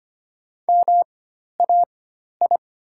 Morse code, heard as a keyed tone, about 25 wpm, spaced as for 12 wpm, 700 Hz. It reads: MAI
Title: 4x Character Spacing - Top 500 Words - 25wpm